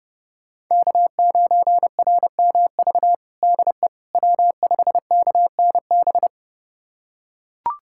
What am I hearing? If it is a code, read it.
Morse code, heard as a keyed tone, about 30 wpm, 700 Hz. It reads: K9RMV DE W5KNB
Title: USA Callsign DE Callsign - 5 Characters - 30wpm